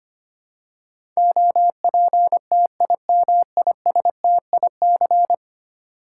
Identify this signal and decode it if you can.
Morse code, heard as a keyed tone, about 25 wpm, 700 Hz. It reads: OPTIMISTIC
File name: Ten Letter Words - 25wpm